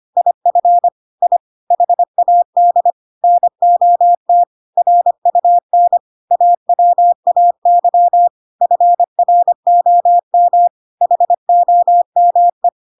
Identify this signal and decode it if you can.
Morse code, heard as a keyed tone, about 25 wpm, 700 Hz. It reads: IF I HAD NOT RUN AWAY FROM HOME
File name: Pinocchio / Pinocchio ch5 25 25 ik1hge